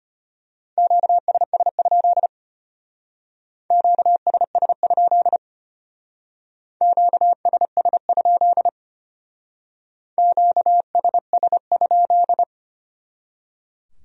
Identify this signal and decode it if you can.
Morse code, heard as a keyed tone, about 32 wpm, 700 Hz. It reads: QSS? QSS? QSS? QSS?